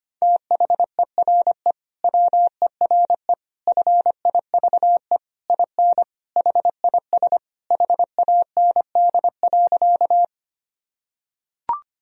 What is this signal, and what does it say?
Morse code, heard as a keyed tone, about 25 wpm, 700 Hz. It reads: THERE WERE FIVE IN HIS HAND.
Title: Sentences from Top 1000 Words - Review Part 2 - 25wpm